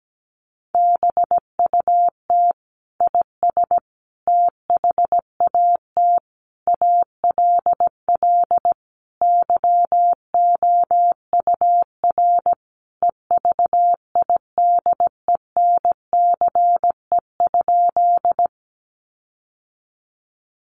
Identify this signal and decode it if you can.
Morse code, heard as a keyed tone, about 17 wpm, 700 Hz. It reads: BUT IS THAT ALL YOUR EVIDENCE?
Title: Sentences from Top 1300 Words 17wpm